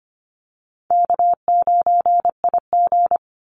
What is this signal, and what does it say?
Morse code, heard as a keyed tone, about 25 wpm, 700 Hz. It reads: K9IG
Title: Call Signs Repeated 2x - 25wpm